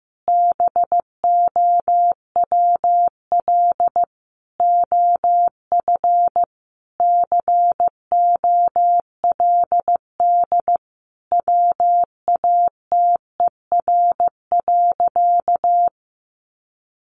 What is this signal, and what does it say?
Morse code, heard as a keyed tone, about 15 wpm, 700 Hz. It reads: BOWL OF COLD WATER.
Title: Sentences from Top 1200 Words 15wpm